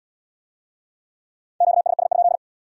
Morse code, heard as a keyed tone, about 66 wpm, 700 Hz. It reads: QSS?